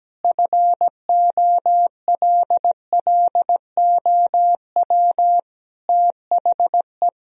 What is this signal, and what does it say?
Morse code, heard as a keyed tone, about 17 wpm, 700 Hz. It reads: FOLLOW THE
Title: Sentences from Top 1000 Words - Review Part 4 - 17wpm